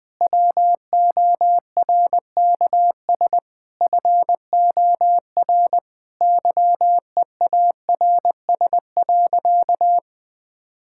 Morse code, heard as a keyed tone, about 20 wpm, 700 Hz. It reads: WORKS FOR YEARS.
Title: Sentences with 6-Word Phrases - 20wpm